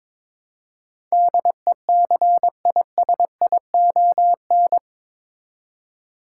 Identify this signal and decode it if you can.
Morse code, heard as a keyed tone, about 22 wpm, 700 Hz. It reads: DECISION
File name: practice / 22wpm / Eight Letter Words - 22wpm